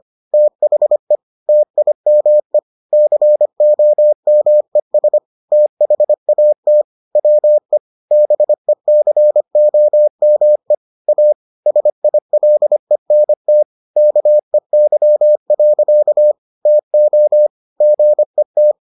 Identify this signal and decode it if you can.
Morse code, heard as a keyed tone, about 25 wpm, 600 Hz. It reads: THE TIME COMES THAT WE BECOME A SILENT KEY. TO GET